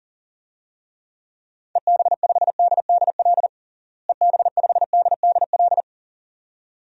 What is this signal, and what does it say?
Morse code, heard as a keyed tone, about 40 wpm, 700 Hz. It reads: EB5DDL EB5DDL
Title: DXCC-100 Call Signs - 6 Characters - Repeated 2x - 40wpm